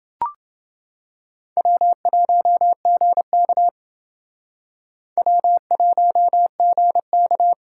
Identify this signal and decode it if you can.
Morse code, heard as a keyed tone, about 29 wpm, 700 Hz. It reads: W1GK W1GK